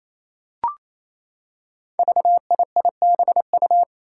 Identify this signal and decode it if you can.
Morse code, heard as a keyed tone, about 28 wpm, 700 Hz. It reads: VIIBU
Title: Five Challenging Letter-Number Combinations 28wpm